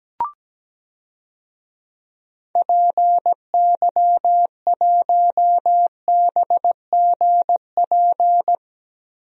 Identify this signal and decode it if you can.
Morse code, heard as a keyed tone, about 17 wpm, 700 Hz. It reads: PY1BGP
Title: DXCC-100 Call Signs - 6 Characters - Repeated 3x - 17wpm